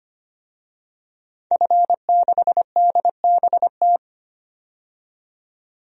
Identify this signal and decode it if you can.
Morse code, heard as a keyed tone, about 25 wpm, 700 Hz. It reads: F6DBT
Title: DXCC-100 Call Signs - 5 Characters - 25wpm